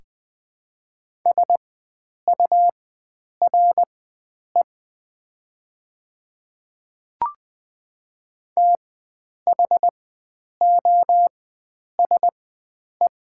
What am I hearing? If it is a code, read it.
Morse code, heard as a keyed tone, about 20 wpm, spaced as for 9 wpm, 700 Hz. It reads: SURE THOSE
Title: 4x Character Spacing - Top 300 Words - 20wpm